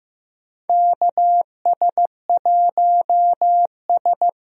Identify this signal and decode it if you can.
Morse code, heard as a keyed tone, about 15 wpm, 700 Hz. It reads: KS1S
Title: USA Call Signs - 4 Characters - Repeated 2x - 15wpm